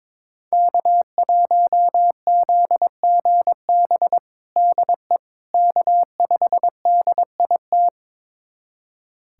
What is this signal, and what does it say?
Morse code, heard as a keyed tone, about 22 wpm, 700 Hz. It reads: K1ZGB DE K5DIT
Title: USA Callsign DE Callsign - 5 Characters - 22wpm